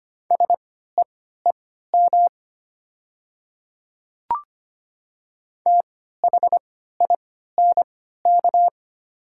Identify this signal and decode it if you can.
Morse code, heard as a keyed tone, about 25 wpm, spaced as for 14 wpm, 690 Hz. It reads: SEEM THINK
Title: 3x Character Spacing - Top 300 Words - 25wpm